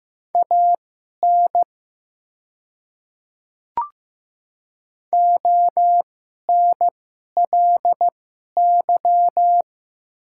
Morse code, heard as a keyed tone, about 15 wpm, spaced as for 11 wpm, 700 Hz. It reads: AN ONLY
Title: 2x Character Spacing - Top 500 Words - 15wpm